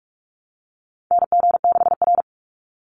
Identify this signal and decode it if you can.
Morse code, heard as a keyed tone, about 44 wpm, 700 Hz. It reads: NG6R
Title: NA SOTA Chaser Callsigns 45wpm